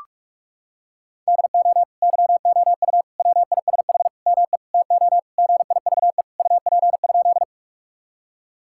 Morse code, heard as a keyed tone, about 46 wpm, 700 Hz. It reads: DO YOU WISH ME TO GIVE UP?
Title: Sentences from Top 1600 Words 45wpm